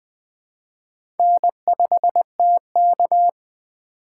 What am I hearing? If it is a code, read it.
Morse code, heard as a keyed tone, about 20 wpm, 700 Hz. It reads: N5TK
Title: USA Call Signs - 4 Characters - 3x Character Spacing - 20wpm